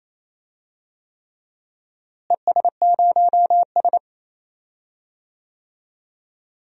Morse code, heard as a keyed tone, about 28 wpm, 700 Hz. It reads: ES0S
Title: DXCC-100 Call Signs - 4 Characters - Repeated 3x - 28wpm